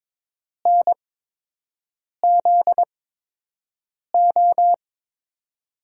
Morse code, heard as a keyed tone, about 22 wpm, spaced as for 6 wpm, 700 Hz. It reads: NZO
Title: USA Call Signs - 6 Characters - 8x Character Spacing - 22wpm